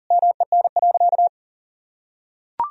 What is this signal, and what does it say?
Morse code, heard as a keyed tone, about 40 wpm, 700 Hz. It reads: MEN.
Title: Sentences from Top 1000 Words - Review Part 3 - 40wpm